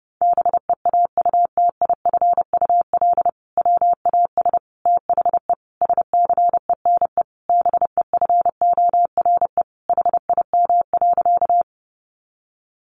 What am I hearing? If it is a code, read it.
Morse code, heard as a keyed tone, about 28 wpm, 700 Hz. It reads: BEAUTIFUL WAS THE SCENE BEFORE HIM.
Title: Sentences from Top 1100 Words 30wpm